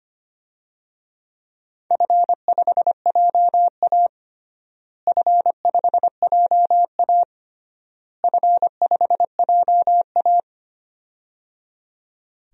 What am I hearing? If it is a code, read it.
Morse code, heard as a keyed tone, about 25 wpm, 700 Hz. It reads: F5JA F5JA F5JA
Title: DXCC-100 Call Signs - 4 Characters - Repeated 3x - 25wpm